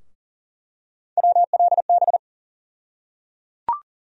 Morse code, heard as a keyed tone, about 37 wpm, 700 Hz. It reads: WLB
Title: USA Call Sign Postfixes - 3 Letters 40wpm